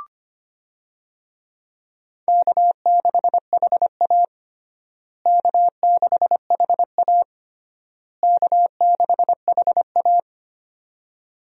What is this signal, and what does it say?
Morse code, heard as a keyed tone, about 25 wpm, 700 Hz. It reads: K6HA K6HA K6HA
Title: USA Call Signs - 4 Characters - Repeated 3x - 25wpm